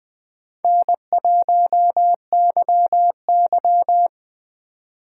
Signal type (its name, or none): Morse code